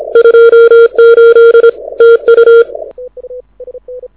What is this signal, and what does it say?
Morse code, heard as a keyed tone, about 26 wpm, 3250 Hz, with noise. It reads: 28 TU
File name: audio zawody / IARU 2015 CW SN7Q